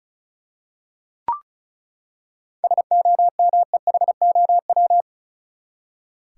Morse code, heard as a keyed tone, about 36 wpm, 700 Hz. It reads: SOMEHOW
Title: Seven Letter Words - 35wpm